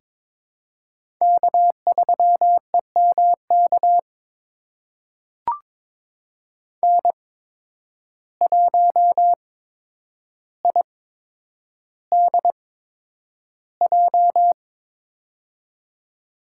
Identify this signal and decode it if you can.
Morse code, heard as a keyed tone, about 22 wpm, spaced as for 9 wpm, 700 Hz. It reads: K3EMK N1IDJ
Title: USA Call Signs - 5 Characters - 8x Character Spacing - 22wpm